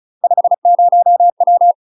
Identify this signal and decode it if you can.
Morse code, heard as a keyed tone, about 36 wpm, 700 Hz. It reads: 50W
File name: UK MASTER QSOs TRAINING / 35WPM / UK MASTER QSOs 0014